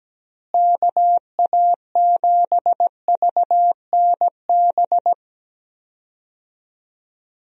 Morse code, heard as a keyed tone, about 17 wpm, 700 Hz. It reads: KA7VNB
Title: USA Call Signs - 6 Characters - Repeated 3x - 17wpm